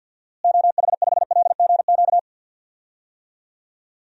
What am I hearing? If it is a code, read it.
Morse code, heard as a keyed tone, about 52 wpm, 700 Hz. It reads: OH5LGQ
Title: DXCC-100 Call Signs - 6 Characters - Repeated 2x - 50wpm